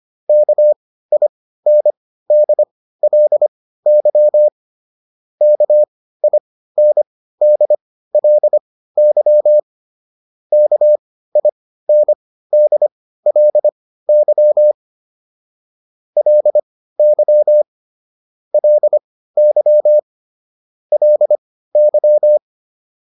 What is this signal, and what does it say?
Morse code, heard as a keyed tone, about 25 wpm, spaced as for 15 wpm, 600 Hz. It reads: KINDLY KINDLY KINDLY LY LY LY